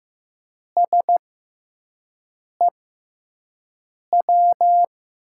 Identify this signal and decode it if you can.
Morse code, heard as a keyed tone, about 15 wpm, spaced as for 5 wpm, 700 Hz. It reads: SEW
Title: USA Call Signs - 5 Characters - 6x Character Spacing - 15wpm